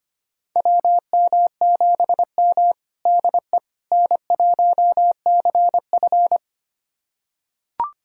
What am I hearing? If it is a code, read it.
Morse code, heard as a keyed tone, about 25 wpm, 700 Hz. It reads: WM7M DE N1CF
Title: USA Callsign DE Callsign - 4 Characters - 25wpm